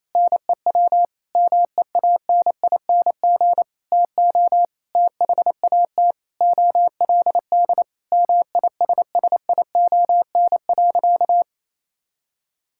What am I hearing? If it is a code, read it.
Morse code, heard as a keyed tone, about 28 wpm, 700 Hz. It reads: NEW MEANING TO THAT OLD MISSION.
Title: Sentences from Top 1600 Words 28wpm